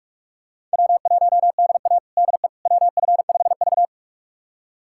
Morse code, heard as a keyed tone, about 46 wpm, 700 Hz. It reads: W1DA DE WF5V
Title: USA Callsign DE Callsign - 4 Characters - 45wpm